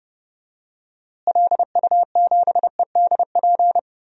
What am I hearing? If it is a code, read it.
Morse code, heard as a keyed tone, about 30 wpm, 700 Hz. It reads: LU7EDP
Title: DXCC-100 Call Signs - 6 Characters - Repeated 2x - 30wpm